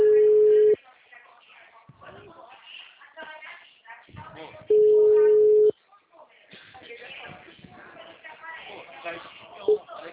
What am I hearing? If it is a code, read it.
Ringback tone (Europe, 425 Hz), with speech.
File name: out- -2010- - - .625